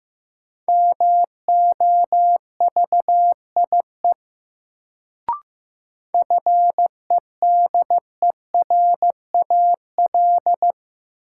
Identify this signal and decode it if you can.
Morse code, heard as a keyed tone, about 15 wpm, 700 Hz. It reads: MOVIE FEDERAL
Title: Sentences from Top 1000 Words - Review Part 5 - New Word Drill - 15wpm